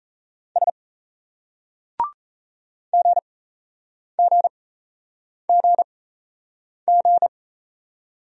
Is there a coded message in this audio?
Yes: Morse code from 0.6 to 7.3 s, heard as a keyed tone, about 38 wpm, spaced as for 8 wpm, 700 Hz. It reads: S GGGG